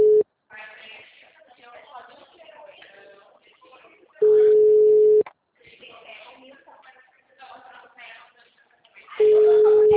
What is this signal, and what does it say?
Ringback tone (Europe, 425 Hz), with speech.
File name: out- -2009- - - .14